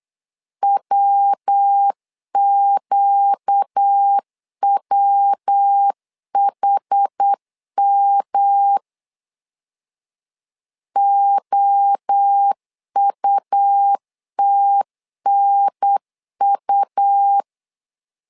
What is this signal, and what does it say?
Morse code, heard as a keyed tone, about 8 wpm, 800 Hz. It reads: WQWHM OUTNU
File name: LEZIONI / 4 Q 8wpm 2732